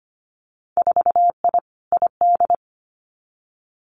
Morse code, heard as a keyed tone, about 25 wpm, 700 Hz. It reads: 4I ID
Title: Winter Field Day Exchanges 25wpm